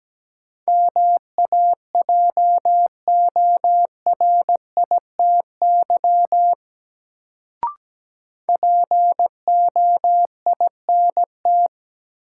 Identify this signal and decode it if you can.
Morse code, heard as a keyed tone, about 17 wpm, 700 Hz. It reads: MAJORITY POINT
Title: Sets of 3 Words - New Word Drill - 17wpm